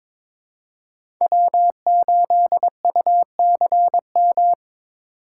Morse code, heard as a keyed tone, about 23 wpm, 700 Hz. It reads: W8UCM
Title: USA Call Signs - 5 Characters - 4x Character Spacing - 22wpm